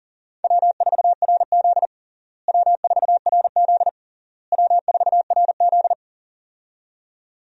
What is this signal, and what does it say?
Morse code, heard as a keyed tone, about 40 wpm, 700 Hz. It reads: W4RZ W4RZ W4RZ
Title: USA Call Signs - 4 Characters - Repeated 3x - 40wpm